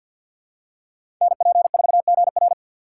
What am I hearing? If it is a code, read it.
Morse code, heard as a keyed tone, about 52 wpm, 690 Hz. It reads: NP4GR
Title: USA Call Signs - 3x Character Spacing - 50wpm